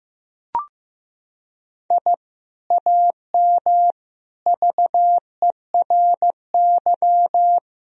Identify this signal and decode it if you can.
Morse code, heard as a keyed tone, about 15 wpm, 700 Hz. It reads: I AM VERY
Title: Sentences from Top 1000 Words - Review Part 3 - 15wpm